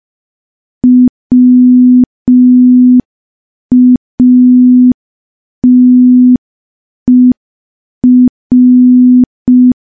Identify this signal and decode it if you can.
Morse code, heard as a keyed tone, about 5 wpm, 250 Hz. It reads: WATER
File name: morse / morse-training-05WPM